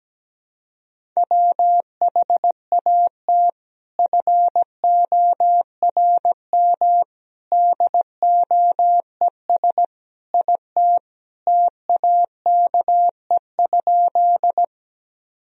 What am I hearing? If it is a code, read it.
Morse code, heard as a keyed tone, about 17 wpm, 700 Hz. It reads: WHAT FORM DOES IT TAKE?
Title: Sentences from Top 1000 Words - Review Part 1 - 17wpm